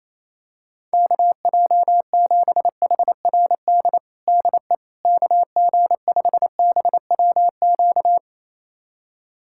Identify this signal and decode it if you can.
Morse code, heard as a keyed tone, about 28 wpm, 700 Hz. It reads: KJ7HRD DE KG5BWQ
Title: USA Callsign DE Callsign 28wpm